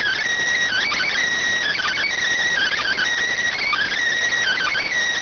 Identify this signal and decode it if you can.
Emergency Alert System (SAME) tones (header bursts), heard as harsh buzzing bursts. It is heard from 0.7 to 4.9 s.